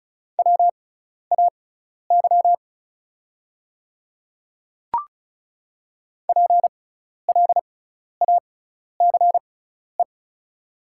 Morse code, heard as a keyed tone, about 34 wpm, spaced as for 12 wpm, 690 Hz. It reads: WAY PLACE
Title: 6x Character Spacing - Top 500 Words - 35wpm